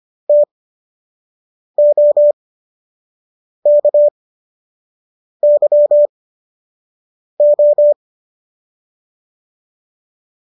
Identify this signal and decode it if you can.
Morse code, heard as a keyed tone, about 25 wpm, spaced as for 6 wpm, 590 Hz. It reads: TOKYO
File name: qso003 06